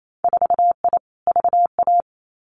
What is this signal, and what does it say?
Morse code, heard as a keyed tone, about 28 wpm, 700 Hz. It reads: 4I VA
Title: Winter Field Day Exchanges 28wpm